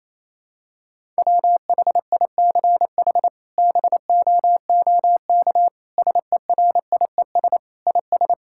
Morse code, heard as a keyed tone, about 27 wpm, 700 Hz. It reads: WHICH BOOK SERIES IS
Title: Sentences from Top 1000 Words - Review Part 5 - 28wpm